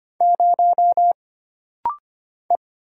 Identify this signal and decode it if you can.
Morse code, heard as a keyed tone, about 25 wpm, 700 Hz. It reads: 0 E
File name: Single Character (Letter:Number:Punctuation:Prosign) - Rapid-Fire - 25wpm